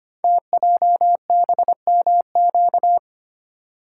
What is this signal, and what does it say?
Morse code, heard as a keyed tone, about 25 wpm, 700 Hz. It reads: TJBMQ
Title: Code Groups - 5 Letters - 25wpm